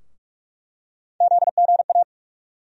Morse code, heard as a keyed tone, about 42 wpm, 700 Hz. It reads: ZGA
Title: USA Call Sign Postfixes - 3 Letters 45wpm